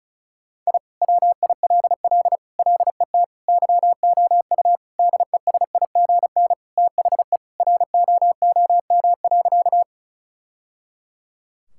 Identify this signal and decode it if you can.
Morse code, heard as a keyed tone, about 36 wpm, 690 Hz. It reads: I WILL LET YOU DESIGN THE ROOM.
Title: Sentences from Top 1000 Words - Review Part 5 - 35wpm